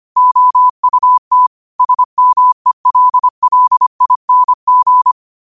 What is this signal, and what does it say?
Morse code, heard as a keyed tone, about 25 wpm, 1000 Hz. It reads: OUT SMELLING
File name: 25wpm